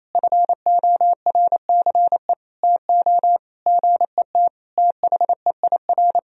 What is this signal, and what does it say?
Morse code, heard as a keyed tone, about 28 wpm, 700 Hz. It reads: FORCE TO GET THEIR
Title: Sentences from Top 1000 Words - Review Part 4 - 28wpm